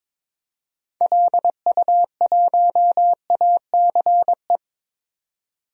Morse code, heard as a keyed tone, about 22 wpm, 700 Hz. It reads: LU1ACE
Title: DXCC-100 Call Signs - 6 Characters - 22wpm